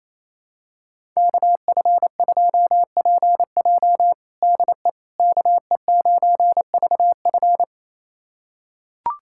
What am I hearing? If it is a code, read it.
Morse code, heard as a keyed tone, about 28 wpm, 700 Hz. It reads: KF2PJ DE KE9VF
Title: USA Callsign DE Callsign - 5 Characters - 28wpm